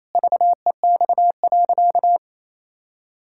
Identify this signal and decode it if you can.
Morse code, heard as a keyed tone, about 28 wpm, 700 Hz. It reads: VEX.
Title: Pangram Challenge v2 - 28wpm